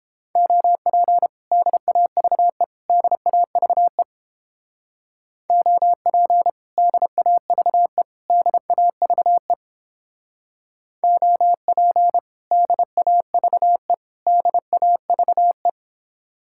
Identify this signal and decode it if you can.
Morse code, heard as a keyed tone, about 31 wpm, 700 Hz. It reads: OP DAVE DAVE OP DAVE DAVE OP DAVE DAVE